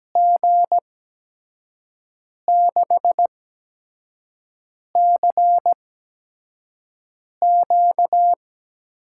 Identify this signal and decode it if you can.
Morse code, heard as a keyed tone, about 17 wpm, spaced as for 5 wpm, 700 Hz. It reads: G6CQ
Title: USA Call Signs - 8x Character Spacing - 17wpm